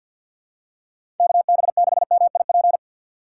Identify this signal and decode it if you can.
Morse code, heard as a keyed tone, about 48 wpm, 700 Hz. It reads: KB6MIP
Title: USA Call Signs - 6 Characters - 3x Character Spacing - 50wpm